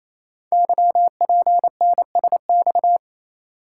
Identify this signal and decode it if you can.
Morse code, heard as a keyed tone, about 27 wpm, 700 Hz. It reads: YPNSX
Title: Code Groups - 5 Letter:Numbers - 28wpm